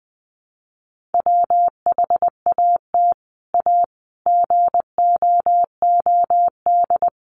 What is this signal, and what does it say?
Morse code, heard as a keyed tone, about 20 wpm, 700 Hz. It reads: WHAT A GOOD
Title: Sentences from Top 200 Words 20wpm